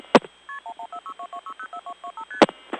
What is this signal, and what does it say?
Multi-frequency trunk signalling, heard as fast pairs of tones, tones of about 90 ms. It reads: KP114622604225ST